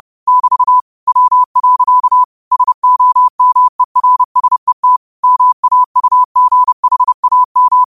Morse code, heard as a keyed tone, about 30 wpm, 1000 Hz. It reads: <BT> W. SOMERSET MAUGHAM
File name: qotdfiles / 30wpm